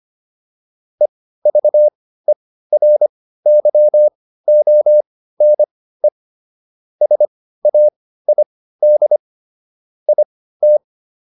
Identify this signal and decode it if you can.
Morse code, heard as a keyed tone, about 25 wpm, spaced as for 15 wpm, 600 Hz. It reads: EVERYONE SAID IT